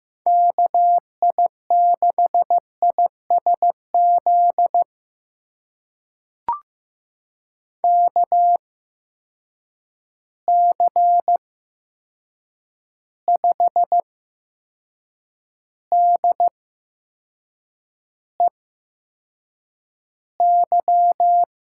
Morse code, heard as a keyed tone, about 15 wpm, spaced as for 6 wpm, 700 Hz. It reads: KI6ISZ KC5DEY
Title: USA Call Signs - 6 Characters - 8x Character Spacing - 15wpm